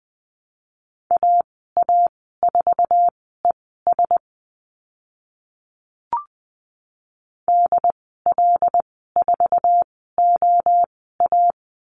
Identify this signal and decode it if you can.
Morse code, heard as a keyed tone, about 20 wpm, spaced as for 14 wpm, 700 Hz. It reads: AA4ES DL4OA
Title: 082 Lesson 14g - Number 2 - Call Signs 20wpm